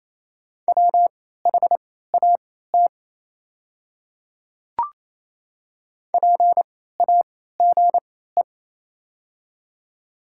Morse code, heard as a keyed tone, about 28 wpm, spaced as for 16 wpm, 700 Hz. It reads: WHAT PAGE
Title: 3x Character Spacing - Top 300 Words - 28wpm